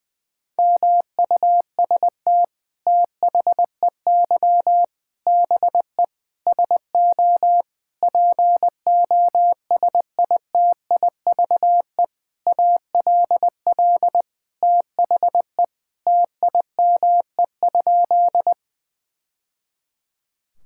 Morse code, heard as a keyed tone, about 20 wpm, 700 Hz. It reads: MUST THEY BE SO POSITIVE ALL THE TIME?
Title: Sentences from Top 1000 Words - Review Part 5 - 20wpm